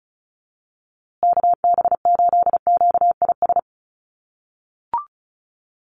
Morse code, heard as a keyed tone, about 34 wpm, 700 Hz. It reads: KB8QIS
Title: USA Call Signs Repeated 3x 35wpm